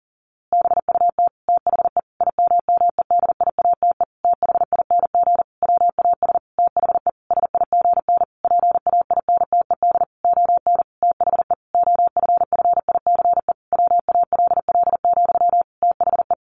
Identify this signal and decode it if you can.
Morse code, heard as a keyed tone, about 40 wpm, 700 Hz. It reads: BUT THE IMMEDIATE THING WAS THE SIGN PAINTED ON THE OFFICE WALL, THE
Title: A Sound Of Thunder 40wpm